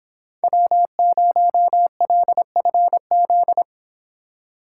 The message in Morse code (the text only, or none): W0LFZ